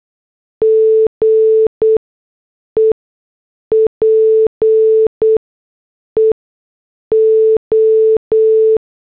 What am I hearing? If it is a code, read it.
Morse code, heard as a keyed tone, about 8 wpm, spaced as for 6 wpm, 440 Hz. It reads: GE PEO